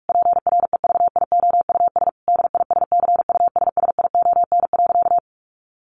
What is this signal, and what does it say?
Morse code, heard as a keyed tone, about 44 wpm, 700 Hz. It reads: PREVIOUS DISCUSSION.
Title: Sentences from Top 1500 Words 45wpm